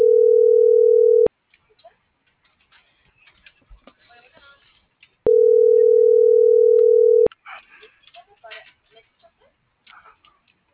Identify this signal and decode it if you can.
Ringback tone (North America), 440 + 480 Hz.